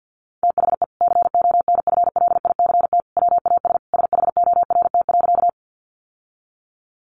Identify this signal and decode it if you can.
Morse code, heard as a keyed tone, about 50 wpm, 700 Hz. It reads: THE CONFLICT WAS SHORT.
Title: Sentences from Top 1300 Words 50wpm